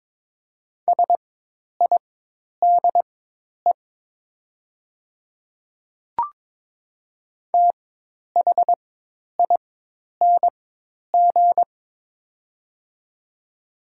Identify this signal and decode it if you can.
Morse code, heard as a keyed tone, about 22 wpm, spaced as for 10 wpm, 700 Hz. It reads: SIDE THING
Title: 4x Character Spacing - Top 100 Words - 22wpm